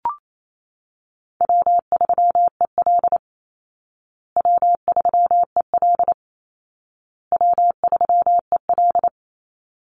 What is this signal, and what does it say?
Morse code, heard as a keyed tone, about 28 wpm, 700 Hz. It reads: W3EL W3EL W3EL
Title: Call Signs Repeated 2x - 28wpm